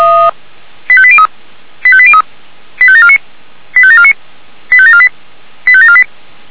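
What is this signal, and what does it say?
Five-tone selective call (ZVEI), heard as quick runs of five tones, tones of about 70 ms. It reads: 85803 85803 85749 85749 85748 85748
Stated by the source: ZVEI five-tone selective call received by SDR, fire brigade Feuerwehr Orlamünde.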